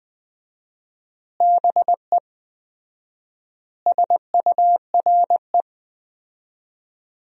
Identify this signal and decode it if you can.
Morse code, heard as a keyed tone, about 20 wpm, 700 Hz. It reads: BE SURE
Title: 4x Word-Spacing - Sentences from Top 300 Words - 20wpm